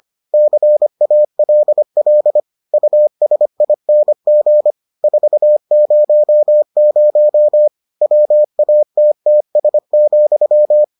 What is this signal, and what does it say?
Morse code, heard as a keyed tone, about 25 wpm, 600 Hz. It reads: CALL USING 400 WATTS,